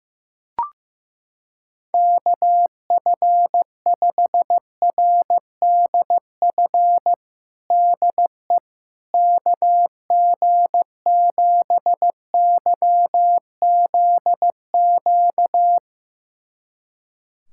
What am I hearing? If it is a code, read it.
Morse code, heard as a keyed tone, about 15 wpm, 700 Hz. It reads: KF5RDF DE KG7YZQ
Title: USA Callsign DE Callsign - 6 Characters - 15wpm